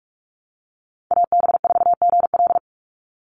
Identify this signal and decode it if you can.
Morse code, heard as a keyed tone, about 44 wpm, 700 Hz. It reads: AB4GL
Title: USA Call Signs 45wpm